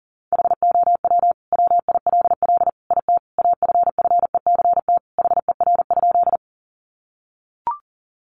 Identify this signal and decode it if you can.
Morse code, heard as a keyed tone, about 40 wpm, 700 Hz. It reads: HOW WILL IT AFFECT HER?
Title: Sentences from Top 900 Words 40wpm